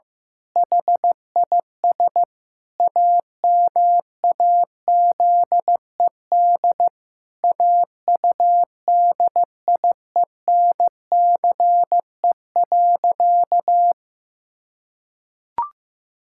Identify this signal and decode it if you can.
Morse code, heard as a keyed tone, about 15 wpm, 700 Hz. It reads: HIS AMAZED AUDIENCE.
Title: Pangram Challenge v2 - 15wpm